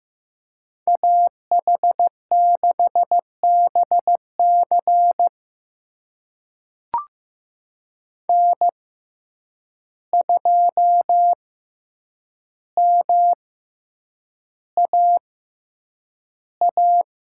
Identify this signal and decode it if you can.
Morse code, heard as a keyed tone, about 15 wpm, spaced as for 8 wpm, 700 Hz. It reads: AH6BC N2MAA
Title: USA Call Signs - 5 Characters - 6x Character Spacing - 15wpm